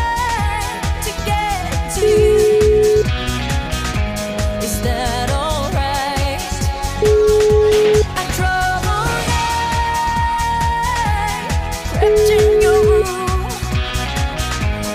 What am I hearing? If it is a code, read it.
Ringback tone (Europe, 425 Hz), with noise.